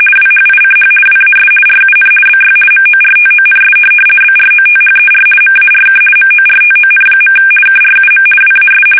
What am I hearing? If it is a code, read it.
RTTY, about 50 baud. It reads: WHICH WAS RUN BY MORTONS BROTHER MARK MORTON. KRUM WAS
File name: fsk850-50B / teletype-corp